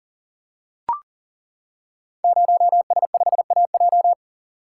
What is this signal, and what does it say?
Morse code, heard as a keyed tone, about 37 wpm, 700 Hz. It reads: 0S5AJ
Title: Code Groups - 5 Letter:Numbers - 40wpm